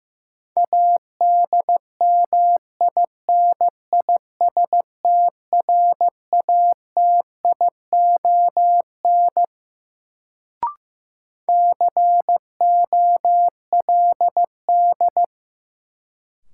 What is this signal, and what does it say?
Morse code, heard as a keyed tone, about 15 wpm, 700 Hz. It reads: ADMINISTRATION COLD
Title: Sets of 5 Words - New Word Drill - 15wpm